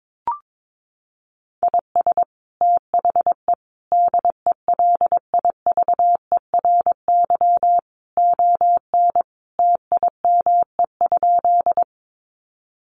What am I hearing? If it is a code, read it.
Morse code, heard as a keyed tone, about 22 wpm, 700 Hz. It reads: IS THE DELIVERY ON TIME?
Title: Sentences from Top 1500 Words 22wpm